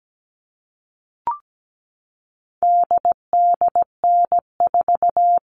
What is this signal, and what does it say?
Morse code, heard as a keyed tone, about 17 wpm, 700 Hz. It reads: DDN4
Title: All Cut Numbers - 4-Digit 17wpm